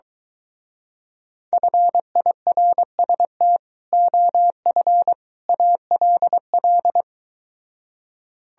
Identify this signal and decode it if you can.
Morse code, heard as a keyed tone, about 23 wpm, 700 Hz. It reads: FIRST OF ALL